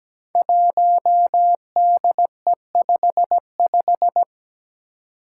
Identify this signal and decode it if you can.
Morse code, heard as a keyed tone, about 17 wpm, 700 Hz. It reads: 1DE55
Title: Five Challenging Letter-Number Combinations 17wpm